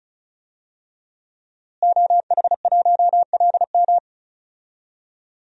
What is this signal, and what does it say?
Morse code, heard as a keyed tone, about 36 wpm, 700 Hz. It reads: OH1LM
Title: DXCC-100 Call Signs - 5 Characters - Repeated 2x - 35wpm